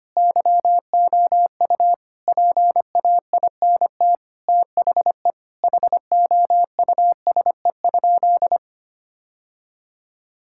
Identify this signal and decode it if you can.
Morse code, heard as a keyed tone, about 25 wpm, 700 Hz. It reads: YOU PAINT THE HOUSE?
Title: Sentences from Top 1000 Words - Review Part 6 - 25wpm